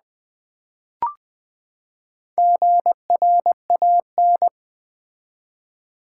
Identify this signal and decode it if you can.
Morse code, heard as a keyed tone, about 20 wpm, 700 Hz. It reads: GRAN
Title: Build A Word - 4 Letters At A Time - 11-Letter Words - 20wpm